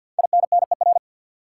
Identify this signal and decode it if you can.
Morse code, heard as a keyed tone, about 53 wpm, 700 Hz. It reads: INNER
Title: Five Letter Words - 50wpm